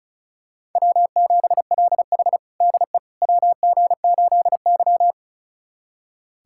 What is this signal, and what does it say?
Morse code, heard as a keyed tone, about 34 wpm, 700 Hz. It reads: W7LH DE WG8Y
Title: USA Callsign DE Callsign - 4 Characters - 35wpm